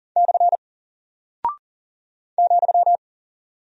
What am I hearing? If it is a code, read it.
Morse code, heard as a keyed tone, about 39 wpm, 700 Hz. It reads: / ,
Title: Single Character (Letter:Number:Punctuation:Prosign) Alt 40wpm